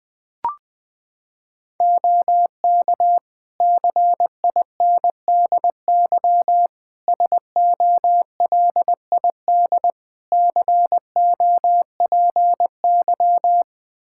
Morse code, heard as a keyed tone, about 20 wpm, 700 Hz. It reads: OK CINDY SOLID COPY
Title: K1USN SST QSO Practice 20wpm